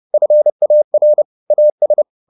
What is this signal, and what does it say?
Morse code, heard as a keyed tone, about 30 wpm, 600 Hz. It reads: FAR AS
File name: the old man and the sea / the old man and the sea 1 30wpm